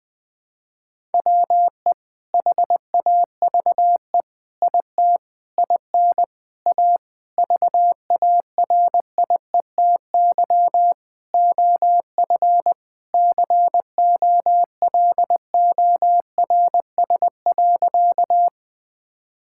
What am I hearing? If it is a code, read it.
Morse code, heard as a keyed tone, about 20 wpm, 700 Hz. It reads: WE HAVE IT IN A VARIETY OF COLORS.
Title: Sentences with 4-Word Phrases - 20wpm